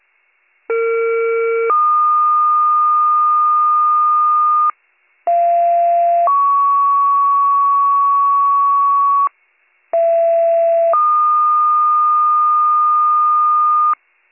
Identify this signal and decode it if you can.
Two-tone page, heard as a one-second tone, then a three-second tone, about 460 then 1220 Hz, about 690 then 1120 Hz, about 670 then 1220 Hz. It is heard from 0.7 to 13.9 s.